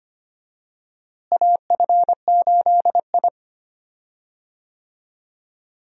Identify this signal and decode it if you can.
Morse code, heard as a keyed tone, about 25 wpm, 700 Hz. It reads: AF8I
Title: USA Call Signs - 4 Characters - Repeated 3x - 25wpm